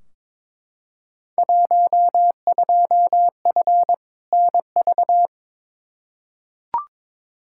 Morse code, heard as a keyed tone, about 22 wpm, 700 Hz. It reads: 12F NV
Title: Field Day Exchanges Encore - 22wpm